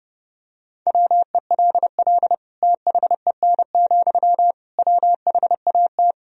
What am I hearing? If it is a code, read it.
Morse code, heard as a keyed tone, about 32 wpm, 700 Hz. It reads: WELL THEN, WHAT
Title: Sentences from Top 200 Words and Abbreviations 30wpm